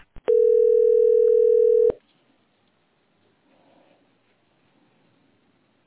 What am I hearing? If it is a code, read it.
Ringback tone (North America), 440 + 480 Hz.